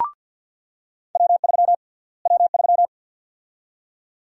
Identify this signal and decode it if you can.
Morse code, heard as a keyed tone, about 49 wpm, 700 Hz. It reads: W3 W3
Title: USA Call Sign Prefixes - Repeated 2x - 50wpm